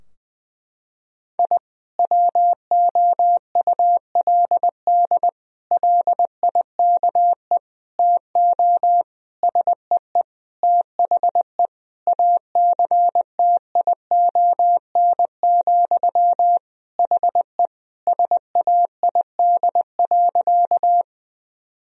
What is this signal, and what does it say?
Morse code, heard as a keyed tone, about 21 wpm, 700 Hz. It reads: I WOULD LIKE TO SEE THE ACTION, HE SAID.
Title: Sentences from Top 1000 Words - Review Part 4 - 20wpm